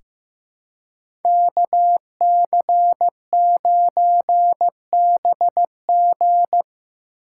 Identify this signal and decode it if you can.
Morse code, heard as a keyed tone, about 15 wpm, 700 Hz. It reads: KC9BG
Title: K1USN SST QSO Practice - 15wpm